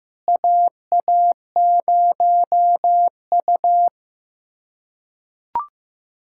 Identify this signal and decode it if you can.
Morse code, heard as a keyed tone, about 15 wpm, 700 Hz. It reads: AA0U
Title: USA Callsign DE Callsign - 4 Characters - 15wpm